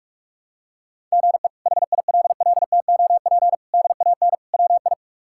Morse code, heard as a keyed tone, about 42 wpm, 700 Hz. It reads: GE HILLTOP DAN WI
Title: K1USN SST QSO Practice - 45wpm